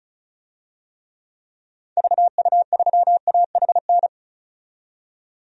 Morse code, heard as a keyed tone, about 34 wpm, 700 Hz. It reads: VU3AHN
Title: DXCC-100 Call Signs - 6 Characters - 35wpm